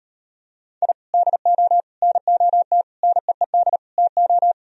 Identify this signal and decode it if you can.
Morse code, heard as a keyed tone, about 38 wpm, 700 Hz. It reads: I DO NOT NEED TO